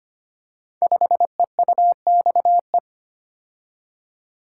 Morse code, heard as a keyed tone, about 25 wpm, 700 Hz. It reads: 5EUXE
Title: Code Groups - 5 Letter:Numbers - 25wpm